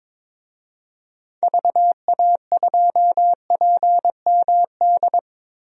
Morse code, heard as a keyed tone, about 22 wpm, 700 Hz. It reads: VA2PMD
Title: DXCC-100 Call Signs - 6 Characters - Repeated 3x - 22wpm